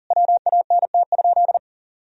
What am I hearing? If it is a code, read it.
Morse code, heard as a keyed tone, about 40 wpm, 700 Hz. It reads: WANT?
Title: Sentences from Top 200 Words and Abbreviations 40wpm